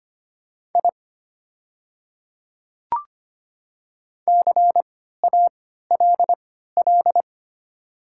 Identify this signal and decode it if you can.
Morse code, heard as a keyed tone, about 25 wpm, spaced as for 14 wpm, 700 Hz. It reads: I CALL
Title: 3x Character Spacing - Top 200 Words - 25wpm